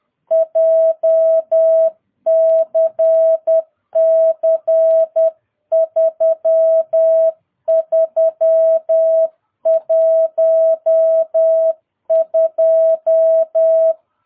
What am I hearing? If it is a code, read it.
Morse code, heard as a keyed tone, about 10 wpm, 650 Hz. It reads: JCC 3312
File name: jh8cbh / 2022cw